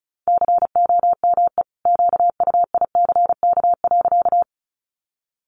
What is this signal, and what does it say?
Morse code, heard as a keyed tone, about 34 wpm, 700 Hz. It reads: COME QUICK.
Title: Sentences from Top 500 Words 35wpm